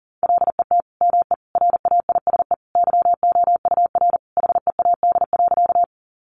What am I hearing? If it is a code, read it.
Morse code, heard as a keyed tone, about 40 wpm, 700 Hz. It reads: LET ME RAISE YOUR HEAD.
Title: Sentences from Top 1400 Words 40wpm